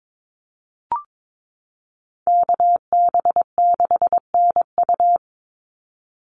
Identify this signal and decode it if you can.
Morse code, heard as a keyed tone, about 22 wpm, 700 Hz. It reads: KB6NU
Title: Call Signs Repeated 3x - 22wpm